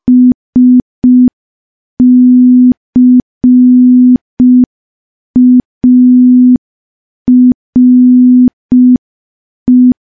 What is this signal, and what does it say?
Morse code, heard as a keyed tone, about 5 wpm, 250 Hz. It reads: SCARE
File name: morse / morse-training-05WPM